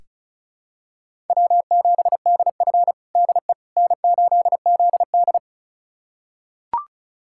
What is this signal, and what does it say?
Morse code, heard as a keyed tone, about 34 wpm, 700 Hz. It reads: W7DF DE N8ZD
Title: USA Callsign DE Callsign - 4 Characters - 35wpm